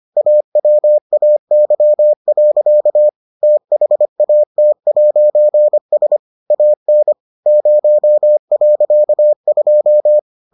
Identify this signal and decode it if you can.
Morse code, heard as a keyed tone, about 25 wpm, 600 Hz. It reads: AWAY. THAT'S AN 0.2